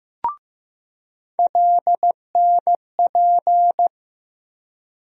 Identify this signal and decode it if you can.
Morse code, heard as a keyed tone, about 15 wpm, 700 Hz. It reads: LNP
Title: USA Call Sign Postfixes - 3 Letters 15wpm